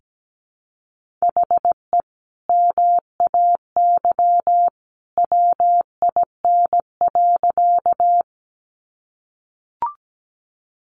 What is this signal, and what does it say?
Morse code, heard as a keyed tone, about 17 wpm, 700 Hz. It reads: HE MAY WIN.
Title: Sentences from Top 1500 Words 17wpm